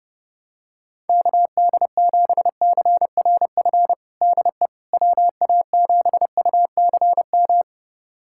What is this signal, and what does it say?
Morse code, heard as a keyed tone, about 30 wpm, 700 Hz. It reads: KD7CRF DE WA7UCM
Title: USA Callsign DE Callsign - 6 Characters - 30wpm